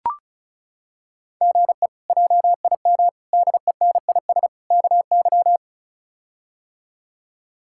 Morse code, heard as a keyed tone, about 36 wpm, 700 Hz. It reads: GE JIM DENIS KY
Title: K1USN SST QSO Practice - 35wpm